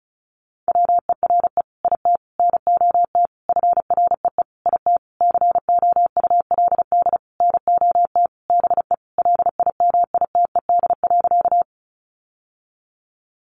Morse code, heard as a keyed tone, about 34 wpm, 700 Hz. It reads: WERE IT NOT FREE IT COULD NOT BE LIMITED.
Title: Sentences from Top 900 Words 35wpm